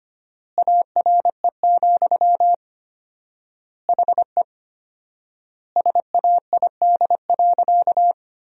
Morse code, heard as a keyed tone, about 25 wpm, 700 Hz. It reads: ARE, HE SAID.
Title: 4x Word-Spacing - Sentences from Top 100 Words - 25wpm